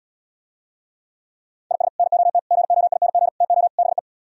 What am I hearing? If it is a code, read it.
Morse code, heard as a keyed tone, about 77 wpm, 700 Hz. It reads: IS NOT GOING TO BE